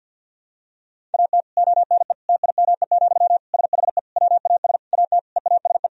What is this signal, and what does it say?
Morse code, heard as a keyed tone, about 51 wpm, 700 Hz. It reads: AT ONE TIME, SHE WAS AT EASE
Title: Sentences from Top 1000 Words - Review Part 1 - 50wpm